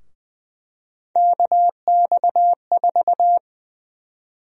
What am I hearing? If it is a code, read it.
Morse code, heard as a keyed tone, about 20 wpm, 700 Hz. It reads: KX4
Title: USA Call Sign Prefixes - Repeated 2x - 20wpm